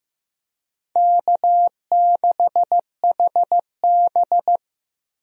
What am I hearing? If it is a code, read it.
Morse code, heard as a keyed tone, about 16 wpm, 700 Hz. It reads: K6HB
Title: USA Call Signs - 4 Characters - Repeated 2x - 15wpm